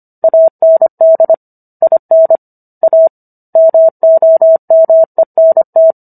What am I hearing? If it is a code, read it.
Morse code, heard as a keyed tone, about 25 wpm, 650 Hz. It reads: AND IN A MOMENT